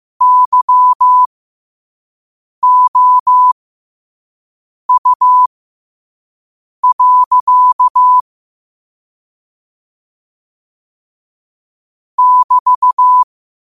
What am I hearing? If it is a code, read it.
Morse code, heard as a keyed tone, about 15 wpm, spaced as for 5 wpm, 1000 Hz. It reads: YOU. <BT>
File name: qotdfiles / 5wpm